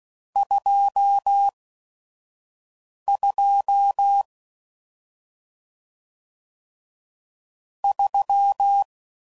Morse code, heard as a keyed tone, about 16 wpm, spaced as for 5 wpm, 800 Hz. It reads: 22 3